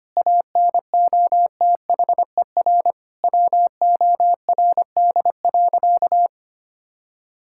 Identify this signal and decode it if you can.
Morse code, heard as a keyed tone, about 25 wpm, 700 Hz. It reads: ANOTHER WORD.
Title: Sentences from Top 1000 Words - Review Part 6 - 25wpm